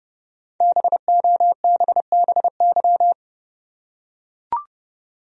Morse code, heard as a keyed tone, about 30 wpm, 700 Hz. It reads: BOBBY
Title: USA Common Names - 30wpm